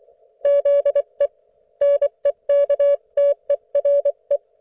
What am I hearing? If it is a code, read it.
Morse code, heard as a keyed tone, about 24 wpm, 570 Hz. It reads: ZE NEKTERE
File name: Archiv / CUC224 / ok1fhi cul224